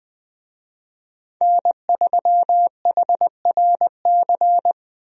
Morse code, heard as a keyed tone, about 20 wpm, 700 Hz. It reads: N3HRC
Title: USA Call Signs - 5 Characters - Repeated 3x - 20wpm